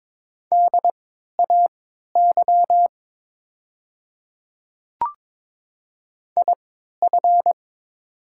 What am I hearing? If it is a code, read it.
Morse code, heard as a keyed tone, about 22 wpm, spaced as for 13 wpm, 690 Hz. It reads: DAY IF
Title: 3x Character Spacing - Top 300 Words - 22wpm